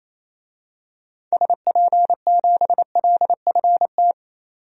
Morse code, heard as a keyed tone, about 27 wpm, 700 Hz. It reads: SP7LFT
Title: DXCC-100 Call Signs - 6 Characters - 28wpm